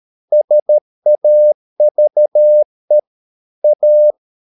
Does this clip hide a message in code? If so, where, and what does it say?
Yes: Morse code from 0.3 to 4.1 s, heard as a keyed tone, about 13 wpm, 600 Hz. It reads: SAVE A